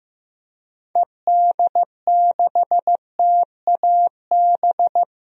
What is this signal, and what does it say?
Morse code, heard as a keyed tone, about 15 wpm, 700 Hz. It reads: ED6TAB
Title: DXCC-100 Call Signs - 6 Characters - Repeated 3x - 15wpm